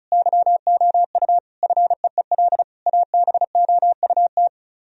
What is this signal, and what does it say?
Morse code, heard as a keyed tone, about 40 wpm, 700 Hz. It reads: YOU FEEL ABOUT